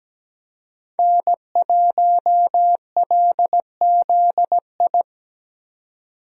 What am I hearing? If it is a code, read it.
Morse code, heard as a keyed tone, about 17 wpm, 700 Hz. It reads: N1LZI
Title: USA Call Signs - 5 Characters - 17wpm